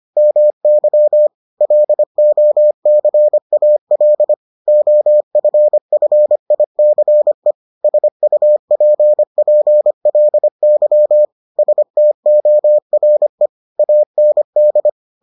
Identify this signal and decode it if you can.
Morse code, heard as a keyed tone, about 25 wpm, 600 Hz. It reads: MY LOCAL OFFICE SUPPLY STORE AND